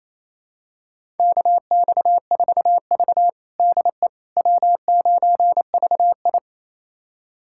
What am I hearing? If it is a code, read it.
Morse code, heard as a keyed tone, about 29 wpm, 700 Hz. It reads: KX4V DE W9VI
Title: USA Callsign DE Callsign - 4 Characters - 28wpm